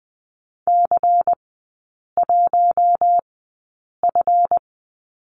Morse code, heard as a keyed tone, about 20 wpm, spaced as for 8 wpm, 700 Hz. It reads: C1F
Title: 138 Lesson 23e - Letter f - Three Characters 20wpm